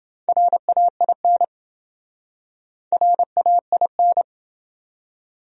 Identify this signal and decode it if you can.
Morse code, heard as a keyed tone, about 28 wpm, 700 Hz. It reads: RAIN RAIN